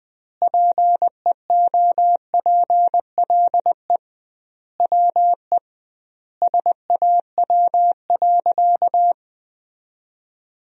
Morse code, heard as a keyed tone, about 20 wpm, 700 Hz. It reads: PEOPLE WE SAW.
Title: 2x Word-Spacing - Sentences from Top 300 Words - 20wpm